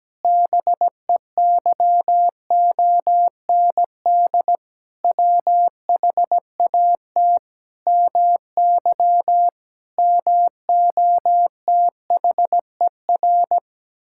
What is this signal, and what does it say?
Morse code, heard as a keyed tone, about 17 wpm, 700 Hz. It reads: BEYOND WHAT MY MOTHER
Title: Sentences from Top 1000 Words - Review Part 6 - 17wpm